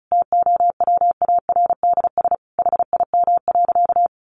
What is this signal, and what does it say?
Morse code, heard as a keyed tone, about 34 wpm, 700 Hz. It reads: TOWARDS HIM.
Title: Sentences from Top 1100 Words 35wpm